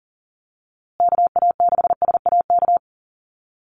Morse code, heard as a keyed tone, about 40 wpm, 700 Hz. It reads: KA6SAK
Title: USA Call Signs Repeated 2x 40wpm